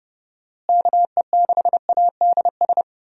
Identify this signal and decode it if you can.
Morse code, heard as a keyed tone, about 30 wpm, 700 Hz. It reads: KE6ADS
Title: USA Call Signs - 6 Characters - Repeated 2x - 30wpm